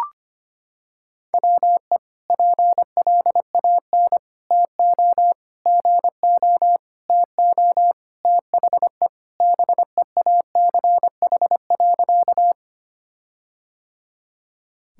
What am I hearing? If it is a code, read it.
Morse code, heard as a keyed tone, about 25 wpm, 700 Hz. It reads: WE PLAN TO GO TO THE BEACH.
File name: practice / 25wpm / Sentences with 4-Word Phrases - 25wpm